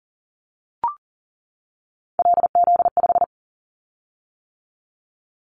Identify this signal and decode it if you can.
Morse code, heard as a keyed tone, about 38 wpm, 700 Hz. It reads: LZ5
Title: Callsign Prefixes for Top 50 DXCC - 40wpm